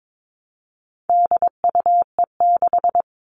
Morse code, heard as a keyed tone, about 22 wpm, 700 Hz. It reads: DUE6
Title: All Cut Numbers - 4-Digit 22wpm